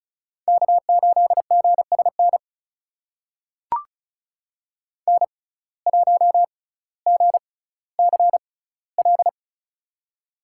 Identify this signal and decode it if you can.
Morse code, heard as a keyed tone, about 34 wpm, spaced as for 18 wpm, 700 Hz. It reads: K8GSN N1GCL
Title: USA Call Signs - 5 Characters - 6x Character Spacing - 35wpm